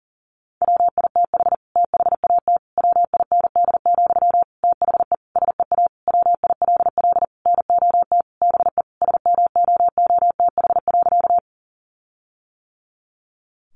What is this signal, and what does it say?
Morse code, heard as a keyed tone, about 40 wpm, 700 Hz. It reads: WITH THAT WIND, THE SEA WILL NOT BE SMOOTH.
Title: Sentences from Top 1400 Words 40wpm